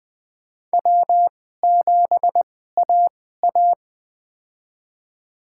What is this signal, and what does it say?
Morse code, heard as a keyed tone, about 20 wpm, spaced as for 14 wpm, 700 Hz. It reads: W7AA
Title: USA Call Signs - 4 Characters - 2x Character Spacing - 20wpm